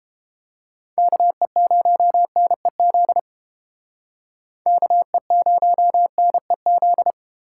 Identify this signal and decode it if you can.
Morse code, heard as a keyed tone, about 30 wpm, 700 Hz. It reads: KE0NEZ KE0NEZ